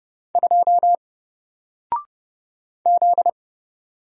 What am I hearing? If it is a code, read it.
Morse code, heard as a keyed tone, about 30 wpm, 700 Hz. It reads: 2 Z
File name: Single-Letter-Number-30wpm